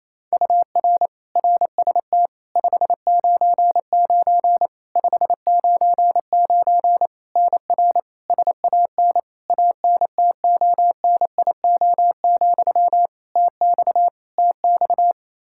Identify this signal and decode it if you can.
Morse code, heard as a keyed tone, about 28 wpm, 700 Hz. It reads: UR RST 599 599 NR SAN ANTONIO, TX TX
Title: K1USN SST QSO Practice 28wpm